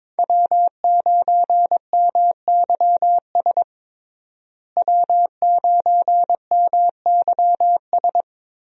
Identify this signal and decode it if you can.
Morse code, heard as a keyed tone, about 22 wpm, 700 Hz. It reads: W9MYS W9MYS
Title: USA Call Signs - 5 Characters - Repeated 2x - 22wpm